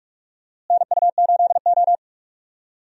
Morse code, heard as a keyed tone, about 42 wpm, 700 Hz. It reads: NU8O
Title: USA Call Signs - 4 Characters - Repeated 3x - 45wpm